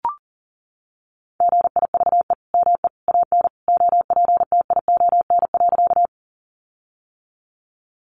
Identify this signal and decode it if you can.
Morse code, heard as a keyed tone, about 40 wpm, 700 Hz. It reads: GIVE ME AN OPTION.
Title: Sentences from Top 1300 Words 40wpm